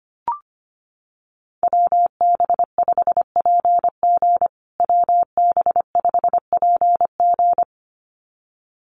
Morse code, heard as a keyed tone, about 25 wpm, 700 Hz. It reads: WB5PG WB5PG
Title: USA Call Signs Repeated 2x 25wpm